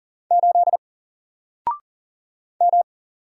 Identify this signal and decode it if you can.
Morse code, heard as a keyed tone, about 40 wpm, 700 Hz. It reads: 8 M
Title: Single Letter-Number Alt 40wpm